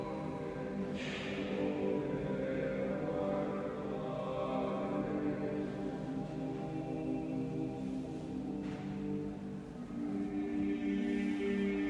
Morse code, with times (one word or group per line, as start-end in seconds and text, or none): none